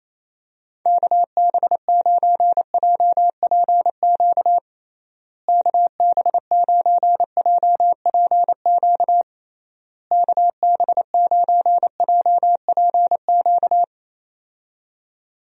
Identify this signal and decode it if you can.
Morse code, heard as a keyed tone, about 28 wpm, 700 Hz. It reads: KB9JPQ KB9JPQ KB9JPQ
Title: USA Call Signs - 6 Characters - Repeated 3x - 28wpm